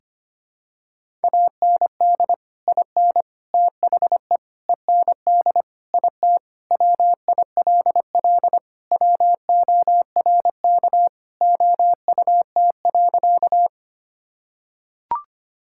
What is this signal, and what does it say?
Morse code, heard as a keyed tone, about 25 wpm, 700 Hz. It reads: AND IN THE END IT WILL WORK OUT.
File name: practice / 25wpm / Sentences with 3-Word Phrases - 25wpm